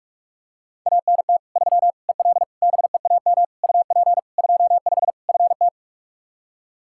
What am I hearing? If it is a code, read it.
Morse code, heard as a keyed tone, about 46 wpm, 700 Hz. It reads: ANT 3 EL BEAM UP 25 FT
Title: Practice QSO - Exchange 2 - Info - 45wpm